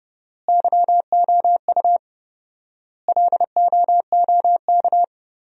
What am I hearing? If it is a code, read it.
Morse code, heard as a keyed tone, about 30 wpm, 700 Hz. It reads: YOU LOOK
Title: 4x Word-Spacing - Sentences from Top 100 Words - 30wpm